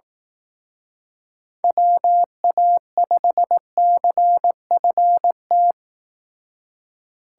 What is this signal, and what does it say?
Morse code, heard as a keyed tone, about 18 wpm, 700 Hz. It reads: WA5CFT
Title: Speed Racing - USA 2x3 Call Signs 15wpm